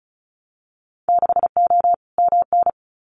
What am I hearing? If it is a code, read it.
Morse code, heard as a keyed tone, about 34 wpm, 700 Hz. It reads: 6O MN
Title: Winter Field Day Exchanges Encore 35wpm